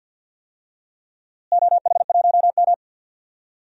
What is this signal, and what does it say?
Morse code, heard as a keyed tone, about 51 wpm, 700 Hz. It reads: OH1M
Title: DXCC-100 Call Signs - 4 Characters - Repeated 3x - 50wpm